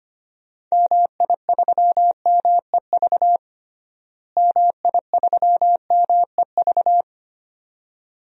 Morse code, heard as a keyed tone, about 25 wpm, 700 Hz. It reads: MI3MEV MI3MEV
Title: DXCC-100 Call Signs - 6 Characters - Repeated 2x - 25wpm